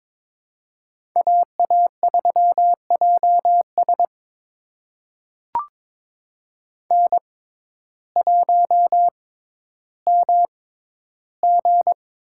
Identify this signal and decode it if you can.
Morse code, heard as a keyed tone, about 22 wpm, spaced as for 12 wpm, 700 Hz. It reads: AA3JS N1MG
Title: USA Call Signs - 6x Character Spacing - 22wpm